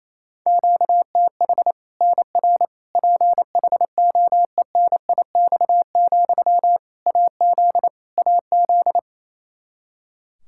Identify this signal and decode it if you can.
Morse code, heard as a keyed tone, about 28 wpm, 700 Hz. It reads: QTH NR PHOENIX, AZ AZ
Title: Practice QSO - Exchange 1 - Beginning - 28wpm